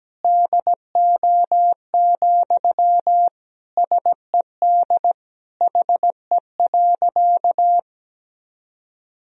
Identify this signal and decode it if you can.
Morse code, heard as a keyed tone, about 17 wpm, 700 Hz. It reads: DO, SED HE.
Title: Sentences from Top 100 Words and Abbreviations 17wpm